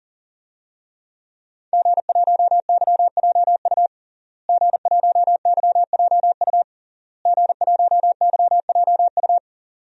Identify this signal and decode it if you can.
Morse code, heard as a keyed tone, about 40 wpm, 700 Hz. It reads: G1YJU G1YJU G1YJU
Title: DXCC-100 Call Signs - 5 Characters - Repeated 3x - 40wpm